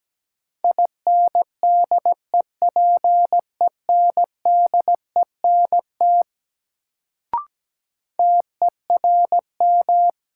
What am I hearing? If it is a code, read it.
Morse code, heard as a keyed tone, about 17 wpm, 700 Hz. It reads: INDEPENDENT TERM
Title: Sentences from Top 900 Words - New Word Drill - 17wpm